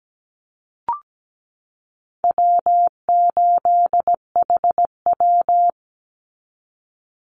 Morse code, heard as a keyed tone, about 17 wpm, 700 Hz. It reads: W8HW
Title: Easy Call Signs Repeated 2x - 17wpm